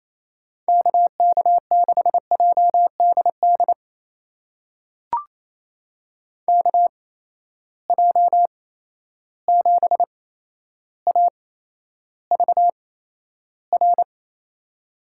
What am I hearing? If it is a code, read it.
Morse code, heard as a keyed tone, about 27 wpm, spaced as for 12 wpm, 700 Hz. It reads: KK6JDD KJ7AVR
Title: USA Call Signs - 6 Characters - 8x Character Spacing - 28wpm